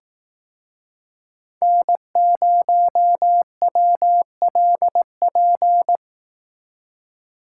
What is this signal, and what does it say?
Morse code, heard as a keyed tone, about 19 wpm, 700 Hz. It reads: N0WLP